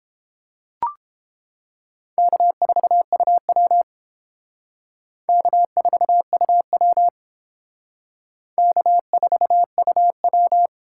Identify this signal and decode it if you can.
Morse code, heard as a keyed tone, about 29 wpm, 700 Hz. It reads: K4UW K4UW K4UW